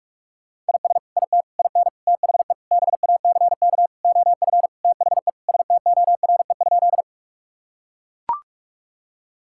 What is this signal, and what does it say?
Morse code, heard as a keyed tone, about 41 wpm, 700 Hz. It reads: IS IT IN THE BACK OF THE STORE?
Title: Sentences from Top 1000 Words - Review Part 5 - 45wpm